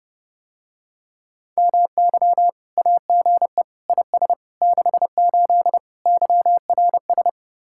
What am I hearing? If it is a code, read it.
Morse code, heard as a keyed tone, about 32 wpm, 700 Hz. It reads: MY AGE IS 68 YRS